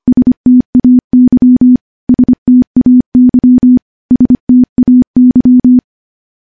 Morse code, heard as a keyed tone, about 25 wpm, 250 Hz. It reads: STAY STAY STAY
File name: morse-training-25WPM